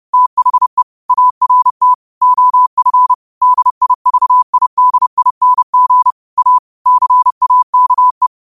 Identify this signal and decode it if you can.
Morse code, heard as a keyed tone, about 30 wpm, 1000 Hz. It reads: THE ART OF DIVIDING A CAKE
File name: qotdfiles / 30wpm